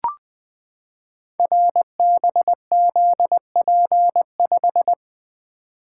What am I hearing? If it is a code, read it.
Morse code, heard as a keyed tone, about 20 wpm, 700 Hz. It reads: RBZP5
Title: Code Groups - 5 Letter:Numbers - 20wpm